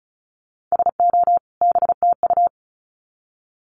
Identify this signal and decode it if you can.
Morse code, heard as a keyed tone, about 34 wpm, 700 Hz. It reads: SO BTU
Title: Common QSO Phrases 35wpm